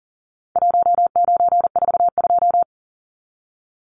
Morse code, heard as a keyed tone, about 40 wpm, 700 Hz. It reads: 1942
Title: Numbers - Contest Style - 4-Digit 40wpm